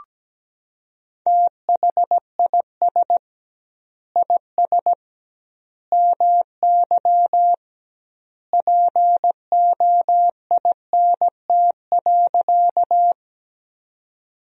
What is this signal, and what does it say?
Morse code, heard as a keyed tone, about 17 wpm, 700 Hz. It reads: THIS IS MY POINT.
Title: 2x Word-Spacing - Sentences from Top 200 Words - 17wpm